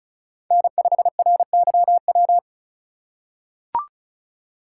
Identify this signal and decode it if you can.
Morse code, heard as a keyed tone, about 36 wpm, 700 Hz. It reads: N5RYW
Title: USA Call Signs - 3x Character Spacing - 35wpm